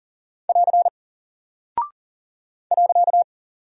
Morse code, heard as a keyed tone, about 40 wpm, 700 Hz. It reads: <AR> .
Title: Single Character (Letter:Number:Punctuation:Prosign) Alt 40wpm